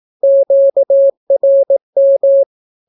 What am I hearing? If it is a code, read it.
Morse code, heard as a keyed tone, about 18 wpm, 550 Hz. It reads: QRM
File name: naukapodstaw / skroty / skroty